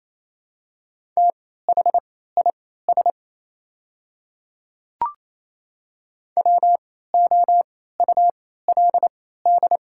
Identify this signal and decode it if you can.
Morse code, heard as a keyed tone, about 27 wpm, spaced as for 16 wpm, 700 Hz. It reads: THIS WOULD
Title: 3x Character Spacing - Top 100 Words - 28wpm